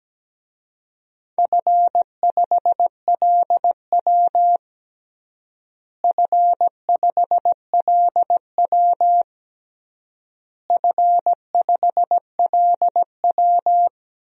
Morse code, heard as a keyed tone, about 17 wpm, 700 Hz. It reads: F5LW F5LW F5LW
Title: DXCC-100 Call Signs - 4 Characters - Repeated 3x - 17wpm